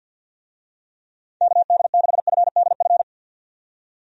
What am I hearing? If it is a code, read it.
Morse code, heard as a keyed tone, about 50 wpm, 700 Hz. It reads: KD6FDF
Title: USA Call Signs - 6 Characters - 50wpm